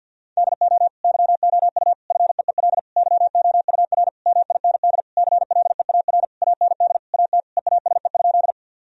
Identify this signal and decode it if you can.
Morse code, heard as a keyed tone, about 46 wpm, 700 Hz. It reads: DO YOU FEEL YOUR MIND CLEAR AND AT EASE?
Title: Sentences from Top 1000 Words - Review Part 1 - 50wpm